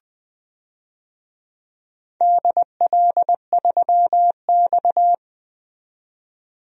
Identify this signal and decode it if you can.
Morse code, heard as a keyed tone, about 20 wpm, 700 Hz. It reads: DL3X
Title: DXCC-100 Call Signs - 4 Characters - 20wpm